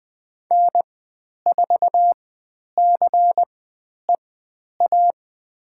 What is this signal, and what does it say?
Morse code, heard as a keyed tone, about 20 wpm, spaced as for 10 wpm, 700 Hz. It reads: N4CEA
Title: 102 Lesson 17g - Letter u - Call Signs 20wpm